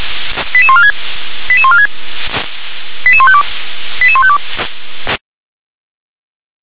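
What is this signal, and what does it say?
Five-tone selective call (ZVEI), heard as quick runs of five tones, tones of about 70 ms. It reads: 88146 88146 88152 88152
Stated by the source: ZVEI five-tone call received by SDR.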